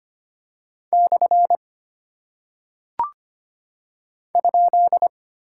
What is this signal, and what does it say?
Morse code, heard as a keyed tone, about 25 wpm, 700 Hz. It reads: / ?
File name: Single Punctuation-Prosign - 25wpm